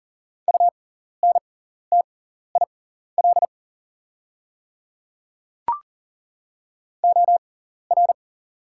Morse code, heard as a keyed tone, about 40 wpm, spaced as for 14 wpm, 700 Hz. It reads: UNTIL OR
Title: 6x Character Spacing - Top 500 Words - 40wpm